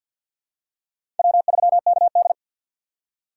Morse code, heard as a keyed tone, about 47 wpm, 690 Hz. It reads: W3KD
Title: USA Call Signs - 4 Characters - 6x Character Spacing - 50wpm